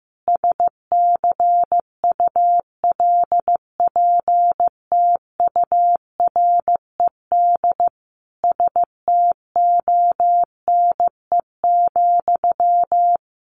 Morse code, heard as a keyed tone, about 15 wpm, 700 Hz. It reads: SCULPTURED STONE,
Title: A Sound Of Thunder 15wpm